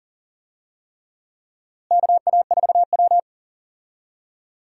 Morse code, heard as a keyed tone, about 40 wpm, 700 Hz. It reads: KA4W
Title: USA Call Signs - 4 Characters - Repeated 3x - 40wpm